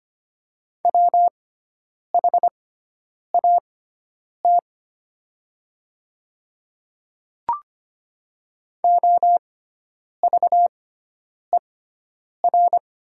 Morse code, heard as a keyed tone, about 25 wpm, spaced as for 9 wpm, 700 Hz. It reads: WHAT OVER
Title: 6x Character Spacing - Top 100 Words - 25wpm